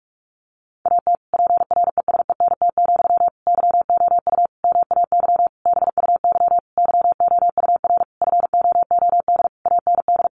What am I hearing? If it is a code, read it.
Morse code, heard as a keyed tone, about 44 wpm, 700 Hz. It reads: AT PRESENT, YOU MAY BUY YOUR FOOD AND
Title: Sentences from Top 800 Words 45wpm